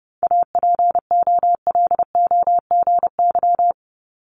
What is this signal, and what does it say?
Morse code, heard as a keyed tone, about 30 wpm, 700 Hz. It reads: APOLOGY
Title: Multi-letter Phonemes Repeated 3x 30wpm